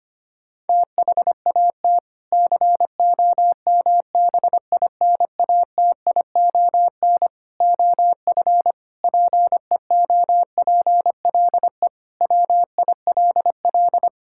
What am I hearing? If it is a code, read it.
Morse code, heard as a keyed tone, about 25 wpm, 700 Hz. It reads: THAT COMBINATION OF PEOPLE WILL
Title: Sentences from Top 1000 Words - Review Part 6 - 25wpm